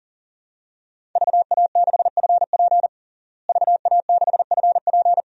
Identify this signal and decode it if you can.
Morse code, heard as a keyed tone, about 40 wpm, 700 Hz. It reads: VA6FP VA6FP
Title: DXCC-100 Call Signs - 5 Characters - Repeated 2x - 40wpm